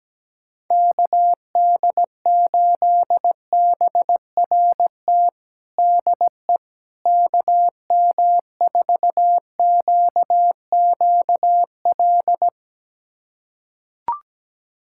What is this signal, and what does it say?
Morse code, heard as a keyed tone, about 17 wpm, 700 Hz. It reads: KD8BRT DE KM4QQL
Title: USA Callsign DE Callsign 17wpm